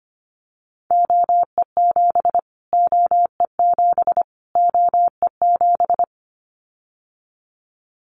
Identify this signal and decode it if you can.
Morse code, heard as a keyed tone, about 25 wpm, 700 Hz. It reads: OE7 OE7 OE7
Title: Callsign Prefixes for Top 25 DXCC - Repeated 3x - 25wpm